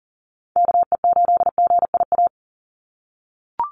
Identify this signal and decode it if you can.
Morse code, heard as a keyed tone, about 37 wpm, 690 Hz. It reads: KE8GIA
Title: USA Call Signs 40wpm